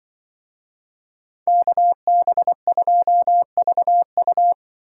Morse code, heard as a keyed tone, about 24 wpm, 700 Hz. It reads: KB2VU
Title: Speed Racing - USA 2x2 Call Signs 20wpm